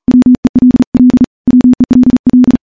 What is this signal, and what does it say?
Morse code, heard as a keyed tone, about 40 wpm, 250 Hz. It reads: WELL WELL
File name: morse-training-40WPM